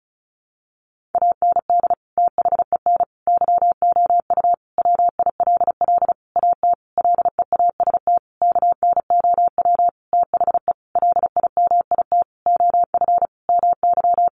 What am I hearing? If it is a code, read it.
Morse code, heard as a keyed tone, about 34 wpm, 700 Hz. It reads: AND THEN YOU WILL AT LEAST KNOW THE LIMIT OF MY
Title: Sentences from Top 1500 Words 35wpm